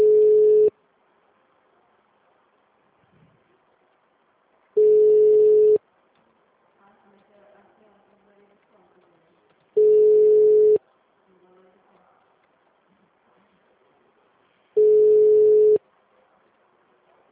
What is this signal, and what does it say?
Ringback tone (Europe, 425 Hz), with speech.
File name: out- -2009- - - .9901